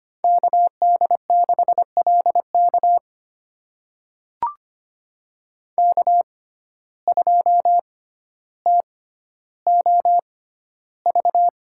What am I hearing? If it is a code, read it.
Morse code, heard as a keyed tone, about 25 wpm, spaced as for 13 wpm, 700 Hz. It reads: KD6LK K2TOV
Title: USA Call Signs - 5 Characters - 6x Character Spacing - 25wpm